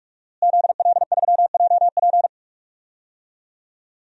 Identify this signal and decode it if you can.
Morse code, heard as a keyed tone, about 46 wpm, 700 Hz. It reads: ZL3JP
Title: DXCC-100 Call Signs - 5 Characters - Repeated 2x - 45wpm